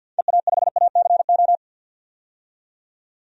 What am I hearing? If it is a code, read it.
Morse code, heard as a keyed tone, about 47 wpm, 700 Hz. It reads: EA5ACO
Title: DXCC-100 Call Signs - 6 Characters - 50wpm